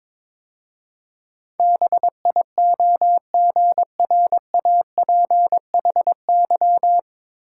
Morse code, heard as a keyed tone, about 23 wpm, 700 Hz. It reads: BIOGRAPHY
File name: 9 Letter Words - First 3 Letters Given - 22wpm